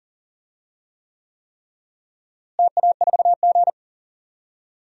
Morse code, heard as a keyed tone, about 40 wpm, 700 Hz. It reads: TA4G
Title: DXCC-100 Call Signs - 4 Characters - Repeated 3x - 40wpm